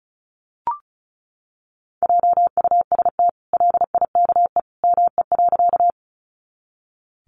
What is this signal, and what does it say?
Morse code, heard as a keyed tone, about 34 wpm, 700 Hz. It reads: JUST LIKE ME.
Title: Sentences from Top 200 Words 35wpm